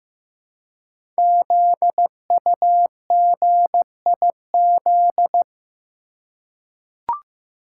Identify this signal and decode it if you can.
Morse code, heard as a keyed tone, about 15 wpm, 700 Hz. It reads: ZUGIZ
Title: Code Groups - 5 Letter:Numbers - 15wpm